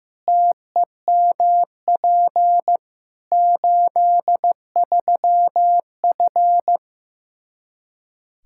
Morse code, heard as a keyed tone, about 15 wpm, 700 Hz. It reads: TEMP 83F
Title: Practice QSO - Exchange 2 - Info - 15wpm